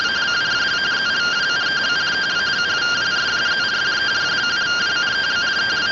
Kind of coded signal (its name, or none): two-tone teleprinter signal